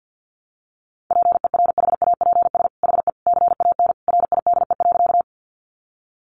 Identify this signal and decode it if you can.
Morse code, heard as a keyed tone, about 50 wpm, 700 Hz. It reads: PERHAPS HE CAN RIDE.
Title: Sentences from Top 1200 Words 50wpm